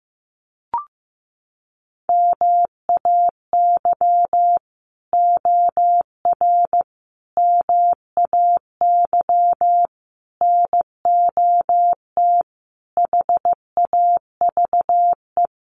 Morse code, heard as a keyed tone, about 15 wpm, 700 Hz. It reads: MAY OR MAY NOT HAVE
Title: Sets of 5 Words Encore2 15wpm